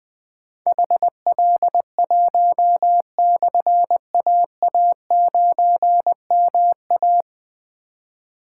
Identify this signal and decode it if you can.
Morse code, heard as a keyed tone, about 21 wpm, 700 Hz. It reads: HL1/AA9MA
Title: Difficult Call Signs 20wpm